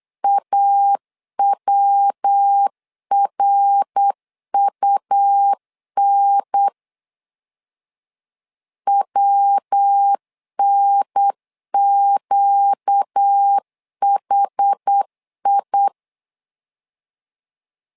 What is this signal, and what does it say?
Morse code, heard as a keyed tone, about 8 wpm, 800 Hz. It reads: AWRUN WNQHI
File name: LEZIONI / 4 Q1S3 8wpm 3037